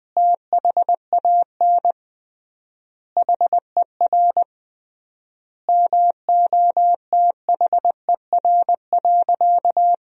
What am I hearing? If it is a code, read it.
Morse code, heard as a keyed tone, about 21 wpm, 700 Hz. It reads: THAN HER MOTHER.
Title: 3x Word-Spacing - Sentences from Top 400 Words - 20wpm